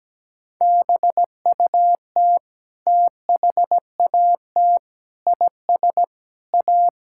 Morse code, heard as a keyed tone, about 17 wpm, 700 Hz. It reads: BUT THAT IS A
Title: Sentences from Top 1000 Words - Review Part 6 - 17wpm